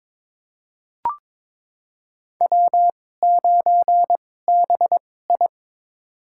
Morse code, heard as a keyed tone, about 22 wpm, spaced as for 16 wpm, 700 Hz. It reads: W9BI
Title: USA Call Signs - 4 Characters - 2x Character Spacing - 22wpm